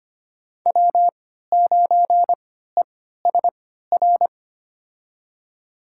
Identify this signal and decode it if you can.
Morse code, heard as a keyed tone, about 25 wpm, spaced as for 14 wpm, 700 Hz. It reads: W9ESR
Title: USA Call Signs - 3x Character Spacing - 25wpm